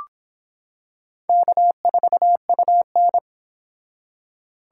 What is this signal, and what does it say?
Morse code, heard as a keyed tone, about 25 wpm, 700 Hz. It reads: K4UN